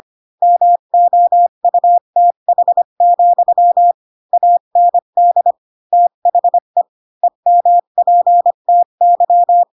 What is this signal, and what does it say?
Morse code, heard as a keyed tone, about 25 wpm, 700 Hz. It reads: MOUTH, AND THE EMPTY
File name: Pinocchio / Pinocchio ch5 20 25 ik1hge